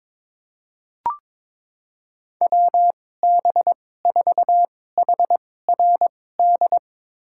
Morse code, heard as a keyed tone, about 22 wpm, spaced as for 16 wpm, 700 Hz. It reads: WB4HRD
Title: USA Call Signs - 6 Characters - 2x Character Spacing - 22wpm